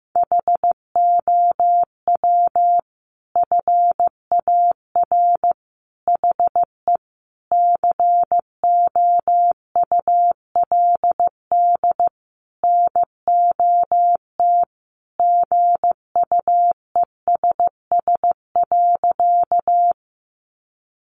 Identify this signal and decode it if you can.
Morse code, heard as a keyed tone, about 15 wpm, 700 Hz. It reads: HOW FAR HE COULD NOT GUESS.
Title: Sentences from Top 1500 Words 15wpm